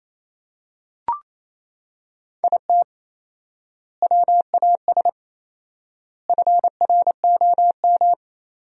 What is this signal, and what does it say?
Morse code, heard as a keyed tone, about 28 wpm, 700 Hz. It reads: IT WAS FROM
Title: 4x Word-Spacing - Sentences from Top 100 Words - 28wpm